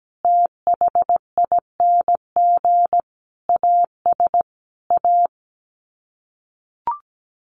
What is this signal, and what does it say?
Morse code, heard as a keyed tone, about 17 wpm, 700 Hz. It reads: THING AS A
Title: Sets of 5 Words Encore 17wpm